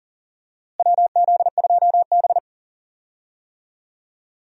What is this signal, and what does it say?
Morse code, heard as a keyed tone, about 40 wpm, 700 Hz. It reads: WZ2B
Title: USA Call Signs - 4 Characters - Repeated 2x - 40wpm